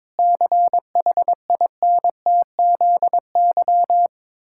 Morse code, heard as a keyed tone, about 22 wpm, 700 Hz. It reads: CHINTZY
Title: Pangram Challenge v2 - 22wpm